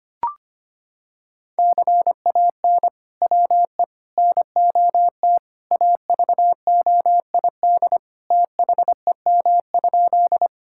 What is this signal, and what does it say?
Morse code, heard as a keyed tone, about 25 wpm, 700 Hz. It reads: CAN WE NOT AVOID THEM?
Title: Sentences from Top 1000 Words - Review Part 6 - 25wpm